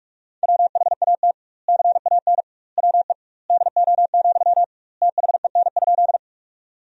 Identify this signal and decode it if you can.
Morse code, heard as a keyed tone, about 41 wpm, 700 Hz. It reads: WHAT CAN WE DO, THEN?
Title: 2x Word-Spacing - Sentences from Top 100 Words - 45wpm